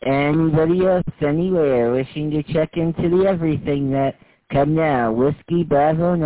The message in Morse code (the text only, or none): none